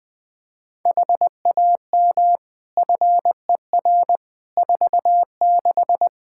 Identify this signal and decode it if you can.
Morse code, heard as a keyed tone, about 20 wpm, 700 Hz. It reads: HAM FER 46
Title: Practice QSO - Exchange 3 - More Info - 20wpm